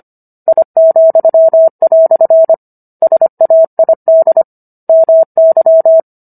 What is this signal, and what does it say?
Morse code, heard as a keyed tone, about 25 wpm, 650 Hz. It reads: I," SAID MY